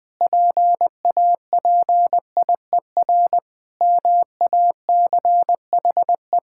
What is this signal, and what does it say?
Morse code, heard as a keyed tone, about 20 wpm, 700 Hz. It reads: PAPIER MACHE
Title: Pangram Challenge v2 - 20wpm